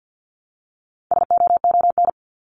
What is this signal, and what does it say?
Morse code, heard as a keyed tone, about 50 wpm, 700 Hz. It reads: SOON
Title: Top 500 Words Repeated 50wpm